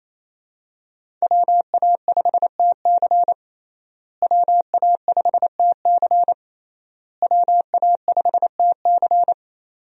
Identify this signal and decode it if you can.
Morse code, heard as a keyed tone, about 28 wpm, 700 Hz. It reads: WA5TC WA5TC WA5TC
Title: USA Call Signs - 5 Characters - Repeated 3x - 28wpm